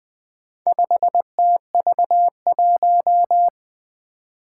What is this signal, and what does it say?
Morse code, heard as a keyed tone, about 20 wpm, 700 Hz. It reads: 5TV1
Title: Four Challenging Letter-Number Combinations 20wpm